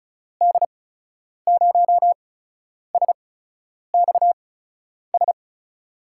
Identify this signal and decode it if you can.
Morse code, heard as a keyed tone, about 36 wpm, spaced as for 10 wpm, 700 Hz. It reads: D0SXS
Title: USA Call Signs - 6 Characters - 8x Character Spacing - 35wpm